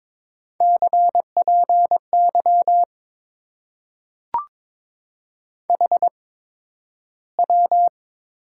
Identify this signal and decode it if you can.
Morse code, heard as a keyed tone, about 22 wpm, spaced as for 12 wpm, 700 Hz. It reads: CPY HW
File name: Top 90 QSO Elements - 8x Character Spacing - 22wpm